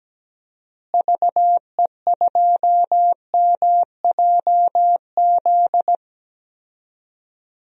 Morse code, heard as a keyed tone, about 17 wpm, 700 Hz. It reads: VE2MJZ
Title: DXCC-100 Call Signs - 6 Characters - Repeated 2x - 17wpm